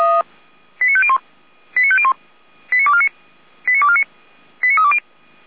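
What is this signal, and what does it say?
Five-tone selective call (ZVEI), heard as quick runs of five tones, tones of about 70 ms. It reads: 89501 89501 89249 89249 89320
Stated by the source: Received by SDR.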